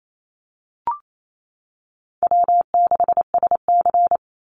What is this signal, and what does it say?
Morse code, heard as a keyed tone, about 28 wpm, 700 Hz. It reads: W6SC
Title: Easy Call Signs - 28wpm